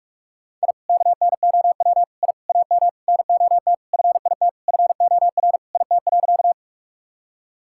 Morse code, heard as a keyed tone, about 46 wpm, 700 Hz. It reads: I KNOW I AM NOT FIT FOR IT.
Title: Sentences from Top 1000 Words - Review Part 5 - 45wpm